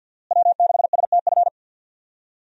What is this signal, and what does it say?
Morse code, heard as a keyed tone, about 47 wpm, 700 Hz. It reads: W6STF
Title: USA Call Signs - 5 Characters - Repeated 2x - 50wpm